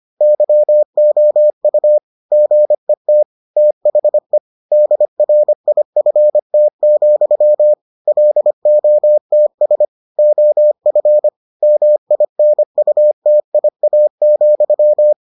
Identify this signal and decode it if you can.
Morse code, heard as a keyed tone, about 25 wpm, 600 Hz. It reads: YOU GET THE DRIFT, LOTS OF MINUTIA,